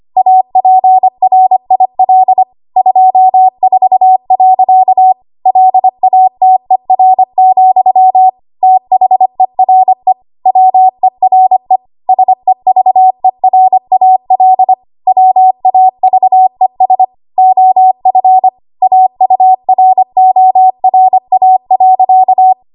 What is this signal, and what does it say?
Morse code, heard as a keyed tone, about 25 wpm, 750 Hz. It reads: APRIL 24. LATER, THERE WERE SEVERAL WAVES OF AURORA.
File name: Morse / 25WPM